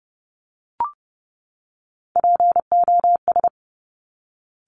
Morse code, heard as a keyed tone, about 30 wpm, 690 Hz. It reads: POS
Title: Top Three Letter Combinations in Words 30wpm